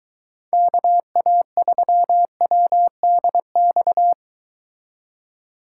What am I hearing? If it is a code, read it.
Morse code, heard as a keyed tone, about 23 wpm, 700 Hz. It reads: KA3WDX